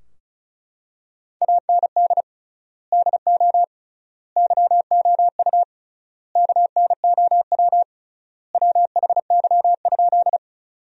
Morse code, heard as a keyed tone, about 34 wpm, 700 Hz. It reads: AND DO YOU KNOW WHY?
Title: 3x Word-Spacing - Sentences from Top 200 Words - 35wpm